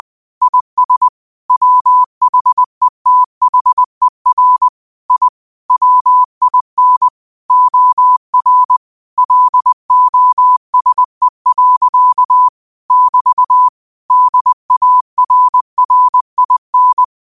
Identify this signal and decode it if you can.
Morse code, heard as a keyed tone, about 20 wpm, 1000 Hz. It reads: IS WHETHER I WIN OR LOSE. <BT> DARRIN